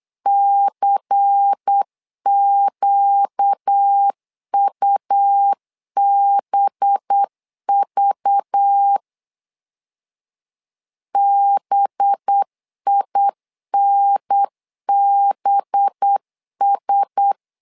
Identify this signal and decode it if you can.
Morse code, heard as a keyed tone, about 8 wpm, 800 Hz. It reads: CQUBV BINBS
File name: LEZIONI / 5 LBVC / 5 LBVC 8wpm 2935